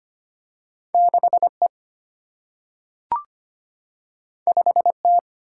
Morse code, heard as a keyed tone, about 25 wpm, 700 Hz. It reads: 6 E 5 T
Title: Single-Letter Single-Number 25wpm